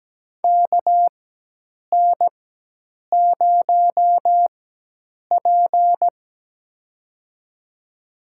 Morse code, heard as a keyed tone, about 17 wpm, spaced as for 8 wpm, 700 Hz. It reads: KN0P
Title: USA Call Signs - 4 Characters - 4x Character Spacing - 17wpm